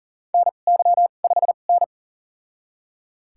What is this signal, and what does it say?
Morse code, heard as a keyed tone, about 40 wpm, spaced as for 29 wpm, 700 Hz. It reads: NY5N
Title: USA Call Signs - 2x Character Spacing - 40wpm